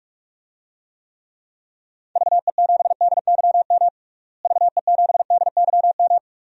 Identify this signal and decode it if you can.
Morse code, heard as a keyed tone, about 46 wpm, 700 Hz. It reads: VE7DYM VE7DYM
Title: DXCC-100 Call Signs - 6 Characters - Repeated 2x - 45wpm